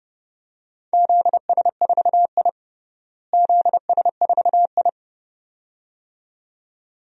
Morse code, heard as a keyed tone, about 30 wpm, 700 Hz. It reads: ZS4I ZS4I
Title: DXCC-100 Call Signs - 4 Characters - Repeated 2x - 30wpm